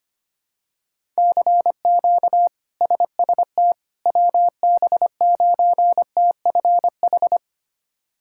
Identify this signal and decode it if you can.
Morse code, heard as a keyed tone, about 25 wpm, 700 Hz. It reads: CQ SST WB9TFH
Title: K1USN SST QSO Practice - 25wpm